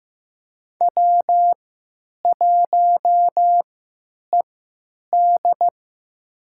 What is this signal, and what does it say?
Morse code, heard as a keyed tone, about 15 wpm, spaced as for 9 wpm, 700 Hz. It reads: W1ED
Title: USA Call Signs - 4 Characters - 3x Character Spacing - 15wpm